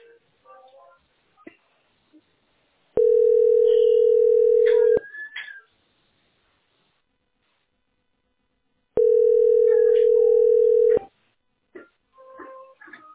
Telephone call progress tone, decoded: ringback tone

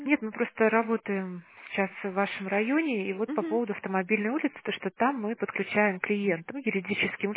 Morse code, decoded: none

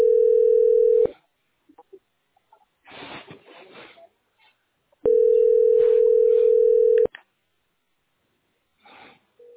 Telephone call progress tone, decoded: ringback tone

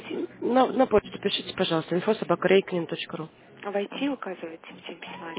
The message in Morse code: none

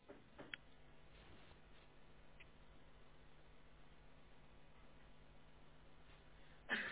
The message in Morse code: none

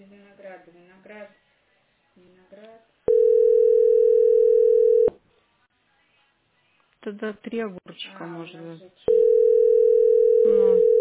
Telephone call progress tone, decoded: ringback tone

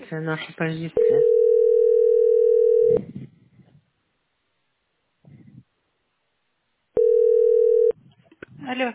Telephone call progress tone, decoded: ringback tone

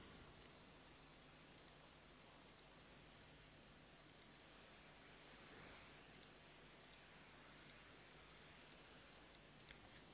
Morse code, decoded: none

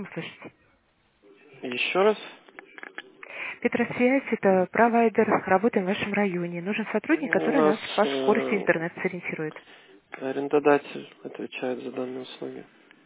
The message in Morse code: none

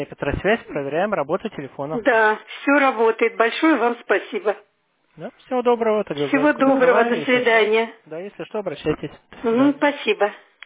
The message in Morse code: none

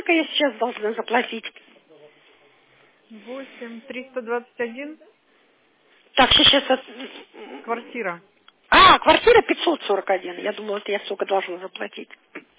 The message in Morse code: none

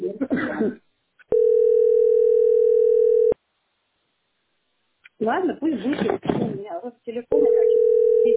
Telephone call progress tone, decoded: ringback tone

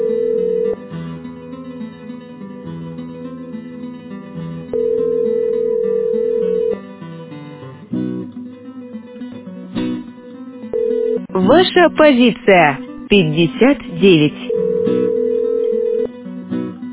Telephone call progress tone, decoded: ringback tone

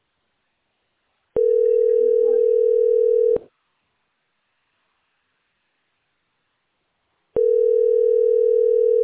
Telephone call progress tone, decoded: ringback tone

